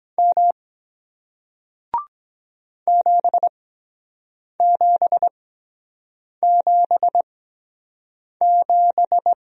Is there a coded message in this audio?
Yes: Morse code from 0.2 to 9.3 s, heard as a keyed tone, about 22 wpm, spaced as for 7 wpm, 700 Hz. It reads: M 7777